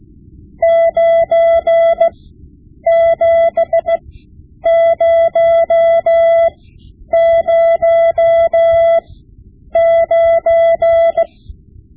Morse code, heard as a keyed tone, about 16 wpm, spaced as for 9 wpm, 670 Hz. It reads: 97009
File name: Morse Lessons / Morse Lessons / Tape-5B